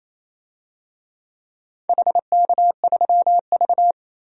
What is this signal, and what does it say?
Morse code, heard as a keyed tone, about 28 wpm, 700 Hz. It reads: HK3V
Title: DXCC-100 Call Signs - 4 Characters - 28wpm